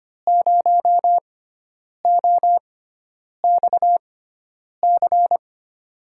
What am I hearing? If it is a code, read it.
Morse code, heard as a keyed tone, about 25 wpm, spaced as for 9 wpm, 700 Hz. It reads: 0OXC
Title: USA Call Signs - 6 Characters - 6x Character Spacing - 25wpm